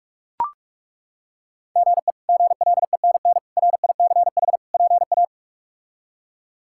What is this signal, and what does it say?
Morse code, heard as a keyed tone, about 43 wpm, 700 Hz. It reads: GE GLENN RICH PA
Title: K1USN SST QSO Practice - 45wpm